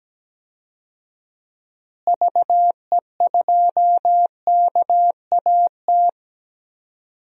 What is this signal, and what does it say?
Morse code, heard as a keyed tone, about 17 wpm, 700 Hz. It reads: VE2KAT
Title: DXCC-100 Call Signs - 6 Characters - Repeated 2x - 17wpm